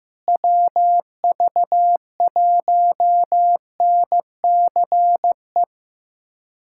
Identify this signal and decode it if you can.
Morse code, heard as a keyed tone, about 15 wpm, 700 Hz. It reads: WV1NCE
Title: USA Callsign DE Callsign - 6 Characters - 15wpm